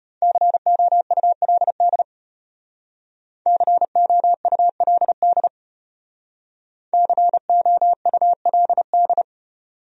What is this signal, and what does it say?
Morse code, heard as a keyed tone, about 32 wpm, 700 Hz. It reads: COULD COULD COULD